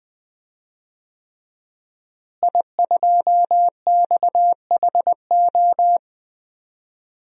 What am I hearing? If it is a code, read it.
Morse code, heard as a keyed tone, about 21 wpm, 700 Hz. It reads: I2XHO
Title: DXCC-100 Call Signs - 5 Characters - Repeated 3x - 20wpm